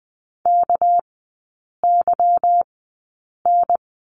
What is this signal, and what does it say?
Morse code, heard as a keyed tone, about 20 wpm, spaced as for 8 wpm, 700 Hz. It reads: KYN
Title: 198 Lesson 32e - Letter k - Three Characters 20wpm